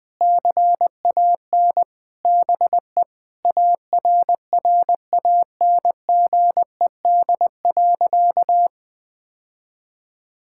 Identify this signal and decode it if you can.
Morse code, heard as a keyed tone, about 21 wpm, 700 Hz. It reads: CAN BE ARRANGED.
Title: Sentences with 3-Word Phrases - 20wpm